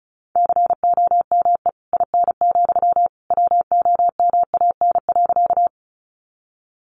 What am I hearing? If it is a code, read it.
Morse code, heard as a keyed tone, about 34 wpm, 700 Hz. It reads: COME IN, WOMAN.
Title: Sentences from Top 1100 Words 35wpm